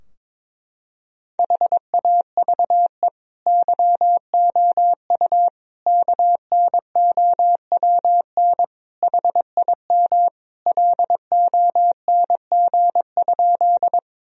Morse code, heard as a keyed tone, about 22 wpm, 700 Hz. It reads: HAVE YOU KNOWN HIM LONG?
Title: Sentences from Top 1000 Words - Review Part 3 - 22wpm